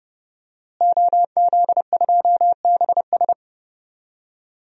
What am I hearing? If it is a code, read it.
Morse code, heard as a keyed tone, about 30 wpm, 700 Hz. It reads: OZ2BS
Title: DXCC-100 Call Signs - 5 Characters - Repeated 2x - 30wpm